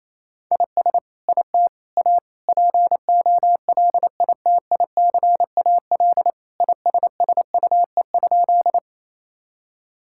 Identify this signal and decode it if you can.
Morse code, heard as a keyed tone, about 28 wpm, 700 Hz. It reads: IS IT A POLITICAL ISSUE?
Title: Sentences from Top 1000 Words - Review Part 6 - 28wpm